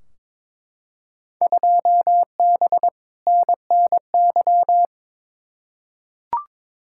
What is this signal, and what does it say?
Morse code, heard as a keyed tone, about 21 wpm, 700 Hz. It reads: 2B NNY
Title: Field Day Exchanges Encore - 22wpm